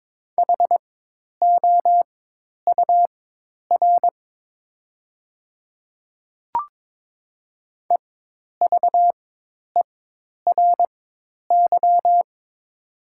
Morse code, heard as a keyed tone, about 22 wpm, spaced as for 10 wpm, 700 Hz. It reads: HOUR EVERY
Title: 4x Character Spacing - Top 500 Words - 22wpm